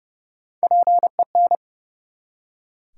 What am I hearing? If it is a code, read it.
Morse code, heard as a keyed tone, about 30 wpm, 700 Hz. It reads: PEN
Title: Three Letter Words - 30wpm